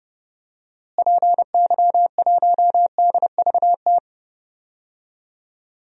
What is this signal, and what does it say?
Morse code, heard as a keyed tone, about 30 wpm, 700 Hz. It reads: PY1DVT
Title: DXCC-100 Call Signs - 6 Characters - Repeated 3x - 30wpm